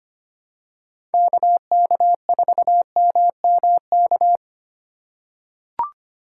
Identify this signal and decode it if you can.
Morse code, heard as a keyed tone, about 25 wpm, 700 Hz. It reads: KK4MMK
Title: USA Call Signs - 6 Characters - 4x Character Spacing - 25wpm